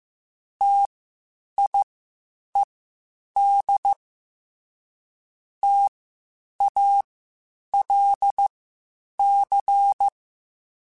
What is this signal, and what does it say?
Morse code, heard as a keyed tone, about 15 wpm, spaced as for 8 wpm, 800 Hz. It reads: TIED TALC
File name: Spaced Alphas / Spaced Alphas mp3s / Ts sp @13